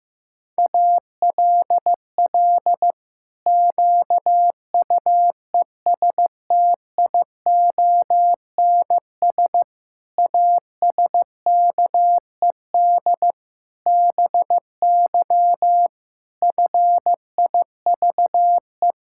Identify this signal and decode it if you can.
Morse code, heard as a keyed tone, about 15 wpm, 700 Hz. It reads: ALL QUESTIONS ASKED BY FIVE
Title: Pangram Challenge v2 - 15wpm